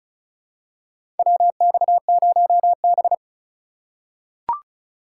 Morse code, heard as a keyed tone, about 36 wpm, 700 Hz. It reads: WX0B
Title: USA Call Signs - 4 Characters - 2x Character Spacing - 35wpm